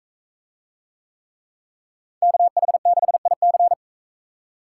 Morse code, heard as a keyed tone, about 40 wpm, 700 Hz. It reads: KH6IC